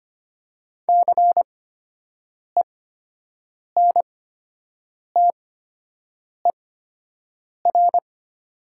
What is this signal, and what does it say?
Morse code, heard as a keyed tone, about 25 wpm, spaced as for 7 wpm, 700 Hz. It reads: CENTER
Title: 8x Character Spacing - Top 500 Words - 25wpm